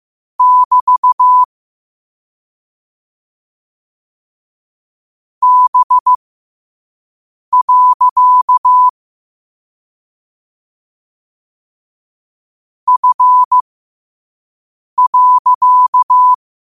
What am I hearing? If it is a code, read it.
Morse code, heard as a keyed tone, about 15 wpm, spaced as for 5 wpm, 1000 Hz. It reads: <BT> B. F.
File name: qotdfiles / 5wpm